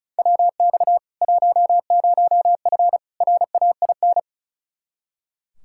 Morse code, heard as a keyed tone, about 34 wpm, 700 Hz. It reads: WX 10F RAIN
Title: K1USN SST QSO Practice 35wpm